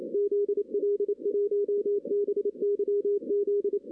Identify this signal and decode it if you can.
Morse code, heard as a keyed tone, about 28 wpm, 410 Hz. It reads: ZL1BYZ
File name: 2016 arrl dx cw / 2016-ARRL-DX-CW-K5TR- - 0-9